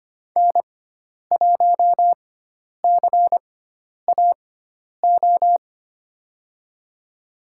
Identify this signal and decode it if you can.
Morse code, heard as a keyed tone, about 25 wpm, spaced as for 10 wpm, 700 Hz. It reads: N1CAO
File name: Lesson 16g - Letter c - Call Signs 25wpm